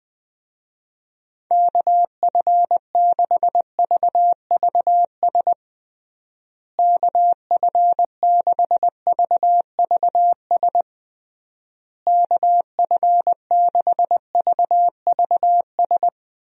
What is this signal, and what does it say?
Morse code, heard as a keyed tone, about 20 wpm, 700 Hz. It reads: KF6VVS KF6VVS KF6VVS
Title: USA Call Signs - 6 Characters - Repeated 3x - 20wpm